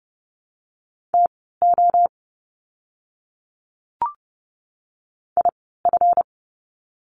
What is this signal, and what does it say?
Morse code, heard as a keyed tone, about 30 wpm, spaced as for 17 wpm, 700 Hz. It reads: TO IF
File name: course / 30:10wpm / 237 Lesson 38f - Letter q - Words 30wpm